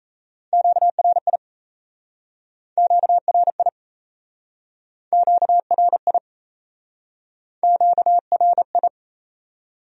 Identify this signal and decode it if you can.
Morse code, heard as a keyed tone, about 38 wpm, 700 Hz. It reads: QRI QRI QRI QRI